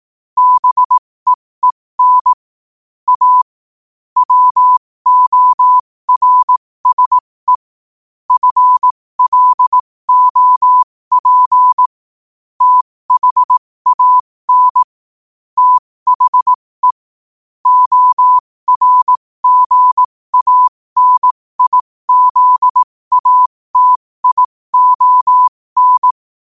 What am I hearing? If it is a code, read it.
Morse code, heard as a keyed tone, about 18 wpm, spaced as for 15 wpm, 1000 Hz. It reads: BEEN A WORSE FLOP THAN THE ORGANIZATION